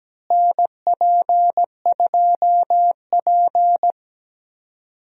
Morse code, heard as a keyed tone, about 17 wpm, 700 Hz. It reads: NP2P
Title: USA Call Signs - 4 Characters - Repeated 3x - 17wpm